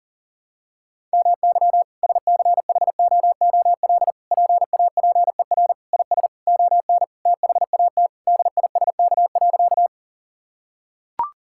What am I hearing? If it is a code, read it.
Morse code, heard as a keyed tone, about 40 wpm, 700 Hz. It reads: MY SCHOOL PAPER IS ON THAT DISK.
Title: Sentences from Top 1000 Words - Review Part 6 - 40wpm